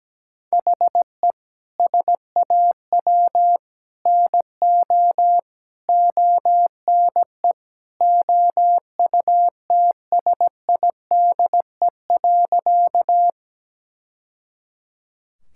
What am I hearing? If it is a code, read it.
Morse code, heard as a keyed tone, about 17 wpm, 700 Hz. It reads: HE SAW NO ONE OUTSIDE.
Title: Sentences from Top 1000 Words - Review Part 6 - 17wpm